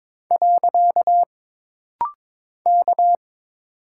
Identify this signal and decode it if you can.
Morse code, heard as a keyed tone, about 22 wpm, 700 Hz. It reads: . K
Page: This rapid-fire timing is geared to push your ICR (Instant Character Recognition) skills.